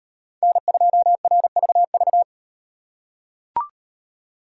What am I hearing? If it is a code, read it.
Morse code, heard as a keyed tone, about 36 wpm, 700 Hz. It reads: N2RVV